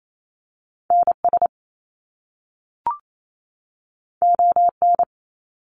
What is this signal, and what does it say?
Morse code, heard as a keyed tone, about 28 wpm, 700 Hz. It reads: NS ON
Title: Canadian Provinces and Territories 28wpm